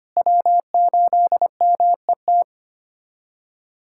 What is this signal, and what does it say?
Morse code, heard as a keyed tone, about 25 wpm, 700 Hz. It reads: W8MET
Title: K1USN SST QSO Practice - 25wpm